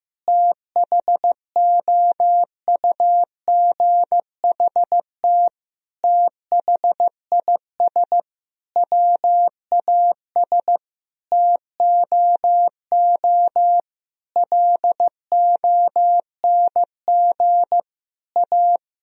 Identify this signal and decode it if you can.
Morse code, heard as a keyed tone, about 16 wpm, 700 Hz. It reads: THOUGHT THIS WAS TOO LONG A
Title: Sentences from Top 1000 Words - Review Part 4 - 15wpm